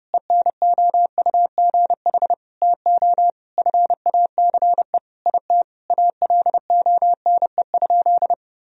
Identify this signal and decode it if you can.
Morse code, heard as a keyed tone, about 30 wpm, 700 Hz. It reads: ENOUGH TO FACE IT ALONE?
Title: Sentences from Top 1000 Words - Review Part 3 - 30wpm